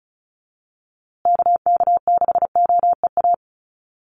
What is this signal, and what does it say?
Morse code, heard as a keyed tone, about 34 wpm, 700 Hz. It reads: KK6OEA
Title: USA Call Signs Repeated 2x 35wpm